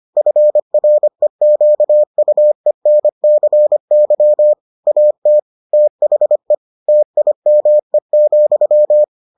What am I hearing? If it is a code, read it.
Morse code, heard as a keyed tone, about 25 wpm, 600 Hz. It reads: FREQUENCY AT THE TIME,